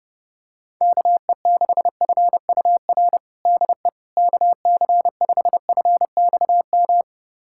Morse code, heard as a keyed tone, about 32 wpm, 700 Hz. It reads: KE6FUR DE KC5FXM
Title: USA Callsign DE Callsign 30wpm